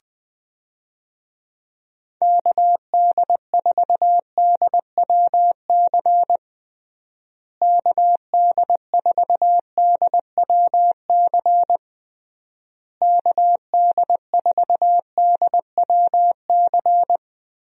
Morse code, heard as a keyed tone, about 21 wpm, 700 Hz. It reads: KD4DWC KD4DWC KD4DWC
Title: USA Call Signs - 6 Characters - Repeated 3x - 20wpm